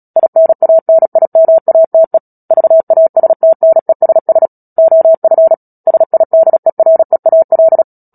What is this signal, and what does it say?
Morse code, heard as a keyed tone, about 37 wpm, 650 Hz. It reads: INANIMATE VASTNESS OF SIDEREAL
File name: Book Two / 10-Chapter 10 - Epilogue - 36 WPM